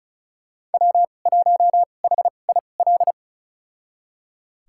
Morse code, heard as a keyed tone, about 36 wpm, spaced as for 25 wpm, 700 Hz. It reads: W1HIL
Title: USA Call Signs - 5 Characters - 2x Character Spacing - 35wpm